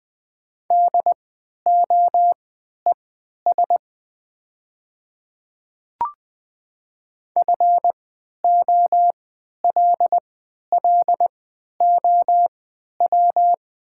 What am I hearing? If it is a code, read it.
Morse code, heard as a keyed tone, about 20 wpm, spaced as for 11 wpm, 700 Hz. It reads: DOES FOLLOW
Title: 3x Character Spacing - Top 200 Words - 20wpm